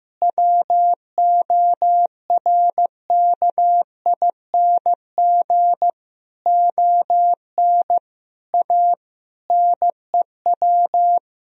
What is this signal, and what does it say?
Morse code, heard as a keyed tone, about 15 wpm, 700 Hz. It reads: WORKING ON A NEW
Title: Sentences from Top 1000 Words - Review Part 4 - 15wpm